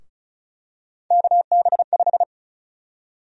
Morse code, heard as a keyed tone, about 34 wpm, 700 Hz. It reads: KB5
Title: USA Call Sign Prefixes - 35wpm